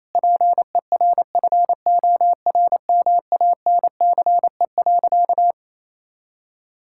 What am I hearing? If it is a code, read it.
Morse code, heard as a keyed tone, about 28 wpm, 700 Hz. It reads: PERFORMANCE.
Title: Sentences from Top 1600 Words 28wpm